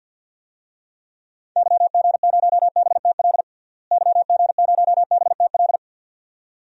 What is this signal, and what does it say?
Morse code, heard as a keyed tone, about 51 wpm, 700 Hz. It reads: YG0BTL YG0BTL
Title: DXCC-100 Call Signs - 6 Characters - Repeated 2x - 50wpm